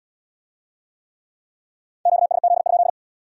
Morse code, heard as a keyed tone, about 70 wpm, 700 Hz. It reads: QSZ?